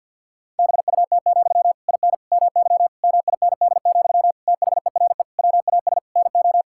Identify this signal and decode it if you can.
Morse code, heard as a keyed tone, about 51 wpm, 690 Hz. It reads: BUT, IN MY MIND, THERE WAS NO
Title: Sentences from Top 1000 Words - Review Part 5 - 50wpm